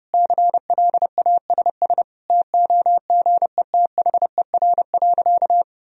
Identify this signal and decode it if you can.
Morse code, heard as a keyed tone, about 32 wpm, 700 Hz. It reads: CLASS TOGETHER.
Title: Sentences from Top 1000 Words - Review Part 2 - 30wpm